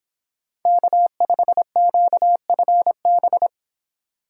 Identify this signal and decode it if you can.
Morse code, heard as a keyed tone, about 26 wpm, 700 Hz. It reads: K5QFB